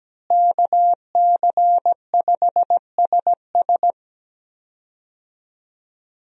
Morse code, heard as a keyed tone, about 17 wpm, 700 Hz. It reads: KC5SS
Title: USA Call Signs - 5 Characters - Repeated 2x - 17wpm